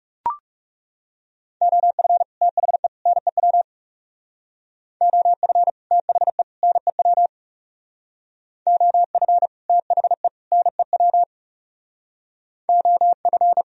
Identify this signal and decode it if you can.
Morse code, heard as a keyed tone, about 36 wpm, 700 Hz. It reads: OF THE NEW OF THE NEW OF THE NEW OF
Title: Speed-Racing - Sets of 3 Words 30wpm